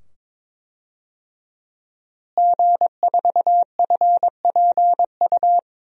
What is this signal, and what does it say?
Morse code, heard as a keyed tone, about 22 wpm, 700 Hz. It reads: G4FPU
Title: DXCC-100 Call Signs - 5 Characters - Repeated 2x - 22wpm